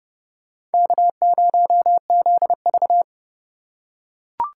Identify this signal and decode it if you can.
Morse code, heard as a keyed tone, about 30 wpm, 700 Hz. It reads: K0ZV
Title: USA Call Signs - 4 Characters - 2x Character Spacing - 30wpm